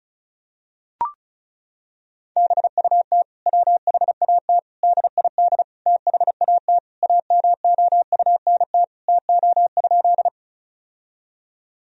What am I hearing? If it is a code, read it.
Morse code, heard as a keyed tone, about 36 wpm, 700 Hz. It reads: BUT WHAT DID THAT AMOUNT TO?
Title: Sentences from Top 1000 Words - Review Part 4 - 35wpm